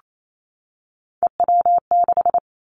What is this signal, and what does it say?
Morse code, heard as a keyed tone, about 28 wpm, 700 Hz. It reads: EW6
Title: Callsign Prefixes for Top 50 DXCC - 28wpm